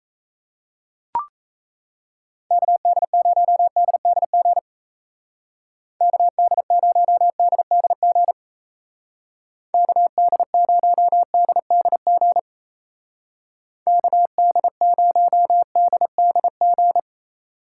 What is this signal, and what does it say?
Morse code, heard as a keyed tone, about 34 wpm, 700 Hz. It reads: KD0DDG KD0DDG KD0DDG KD0DDG
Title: Speed Racing - USA 2x3 Call Signs 28wpm